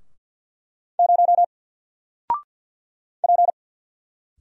Morse code, heard as a keyed tone, about 50 wpm, 700 Hz. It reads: 0 P
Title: Single Character (Letter:Number:Punctuation:Prosign) 50wpm